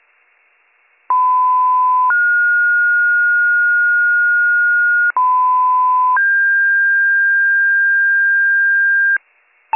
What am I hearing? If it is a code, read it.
Two-tone page, heard as a one-second tone, then a three-second tone, about 1010 then 1430 Hz, about 1010 then 1600 Hz, with noise. It is heard from 1.1 to 9.2 s.